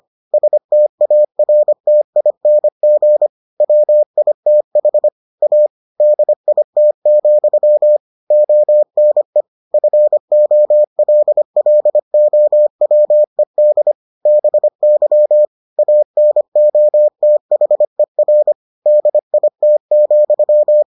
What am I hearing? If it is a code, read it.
Morse code, heard as a keyed tone, about 25 wpm, 600 Hz. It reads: STARTING WITH A DIT, ONE FOLLOWED BY ANOTHER DIT,